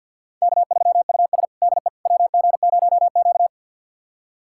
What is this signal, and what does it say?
Morse code, heard as a keyed tone, about 47 wpm, 690 Hz. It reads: K3US DE WG0Q
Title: USA Callsign DE Callsign - 4 Characters - 50wpm